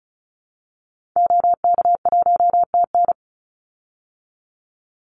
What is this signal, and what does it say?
Morse code, heard as a keyed tone, about 34 wpm, 700 Hz. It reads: OK1TN
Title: Call Signs Repeated 2x - 35wpm